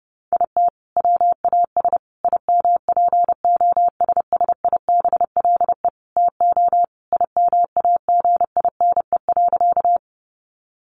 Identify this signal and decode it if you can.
Morse code, heard as a keyed tone, about 30 wpm, 700 Hz. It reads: IT WAS IMPOSSIBLE TO IMAGINE.
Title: Sentences from Top 1400 Words 30wpm